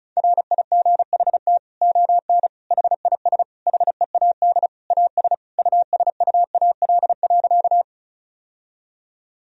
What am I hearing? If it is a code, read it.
Morse code, heard as a keyed tone, about 36 wpm, 700 Hz. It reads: RIGHT ON HIS HEAD AS USUAL.
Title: Sentences from Top 1000 Words - Review Part 2 - 35wpm